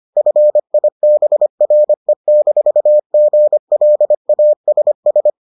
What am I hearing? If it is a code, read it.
Morse code, heard as a keyed tone, about 25 wpm, 600 Hz. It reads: FIBRE-GLASS